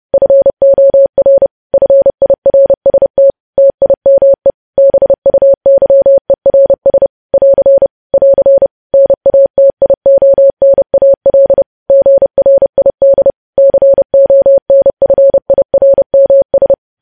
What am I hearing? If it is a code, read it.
Morse code, heard as a keyed tone, about 30 wpm, 550 Hz. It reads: FOR FIRST TIME BUYERS <AR> <AR> NATIONAL GRID CONFIRMS
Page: news headlines in morse code at 30 wpm